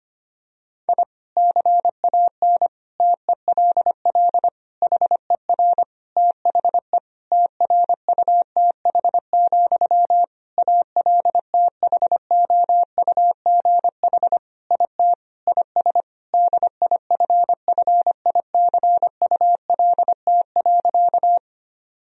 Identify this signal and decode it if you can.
Morse code, heard as a keyed tone, about 25 wpm, 700 Hz. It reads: I CAN TELL HER THE TRUTH, ALTHOUGH IT IS DIFFICULT.
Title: Sentences from Top 1000 Words - Review Part 3 - 25wpm